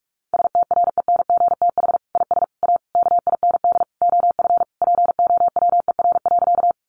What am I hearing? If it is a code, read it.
Morse code, heard as a keyed tone, about 44 wpm, 700 Hz. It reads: STRENGTH IS A KIND OF POWER.
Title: Sentences from Top 1300 Words 45wpm